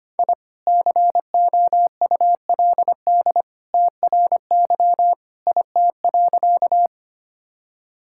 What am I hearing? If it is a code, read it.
Morse code, heard as a keyed tone, about 25 wpm, 700 Hz. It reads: I COULD TRY IT.
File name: practice / 25wpm / Sentences with 2-Word Phrases - 25wpm